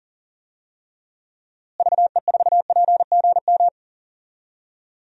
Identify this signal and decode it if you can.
Morse code, heard as a keyed tone, about 40 wpm, 700 Hz. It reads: VE4PGM
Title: DXCC-100 Call Signs - 6 Characters - Repeated 2x - 40wpm